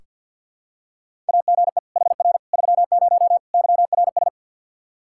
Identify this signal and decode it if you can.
Morse code, heard as a keyed tone, about 46 wpm, 700 Hz. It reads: AGE HR 30 YRS
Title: Practice QSO - Exchange 3 - More Info - 50wpm